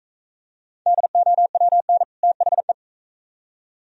Morse code, heard as a keyed tone, about 41 wpm, 700 Hz. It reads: DOWN THE